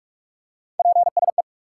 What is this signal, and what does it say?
Morse code, heard as a keyed tone, about 46 wpm, 700 Hz. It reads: PSE
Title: Top 180 QSO Elements 45wpm